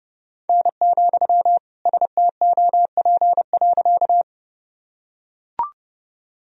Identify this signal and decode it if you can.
Morse code, heard as a keyed tone, about 30 wpm, 700 Hz. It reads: N, STOP.
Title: Sentences from Top 300 Words and Abbreviations 30wpm